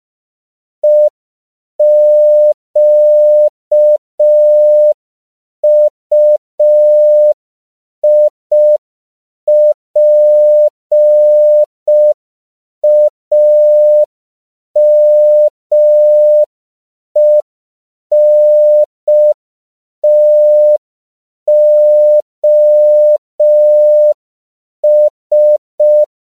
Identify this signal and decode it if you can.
Morse code, heard as a keyed tone, about 5 wpm, 600 Hz. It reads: EQUIPAMENTOS
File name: CW / 5PPM